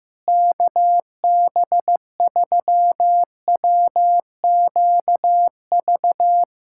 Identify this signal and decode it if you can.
Morse code, heard as a keyed tone, about 16 wpm, 700 Hz. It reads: KB3WQV
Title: USA Callsign DE Callsign 15wpm